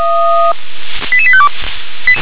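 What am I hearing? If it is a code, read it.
Five-tone selective call (ZVEI), heard as quick runs of five tones, tones of about 70 ms. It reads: 88852
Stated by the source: Five-tone ZVEI selective call received on an SDR.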